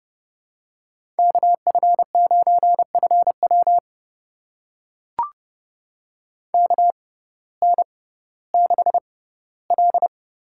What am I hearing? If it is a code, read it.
Morse code, heard as a keyed tone, about 31 wpm, spaced as for 17 wpm, 700 Hz. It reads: KF9FW KN6L
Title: USA Call Signs - 5 Characters - 6x Character Spacing - 30wpm